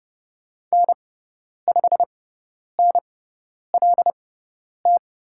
Morse code, heard as a keyed tone, about 30 wpm, spaced as for 10 wpm, 700 Hz. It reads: N5NLT
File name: Lesson 18g - Letter m - Call Signs 30wpm